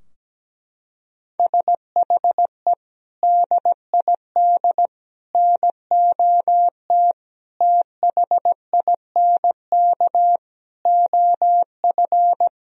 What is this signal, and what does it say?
Morse code, heard as a keyed tone, about 17 wpm, 700 Hz. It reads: SHE DID NOT THINK OF
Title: Sentences from Top 1000 Words - Review Part 3 - 17wpm